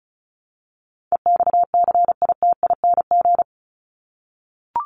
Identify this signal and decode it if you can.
Morse code, heard as a keyed tone, about 34 wpm, 700 Hz. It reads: EXCITING
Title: Top 1751-2000 Words 35wpm